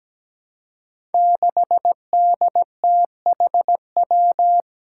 Morse code, heard as a keyed tone, about 17 wpm, 700 Hz. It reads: 6DTHW
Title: Five Challenging Letter-Number Combinations 17wpm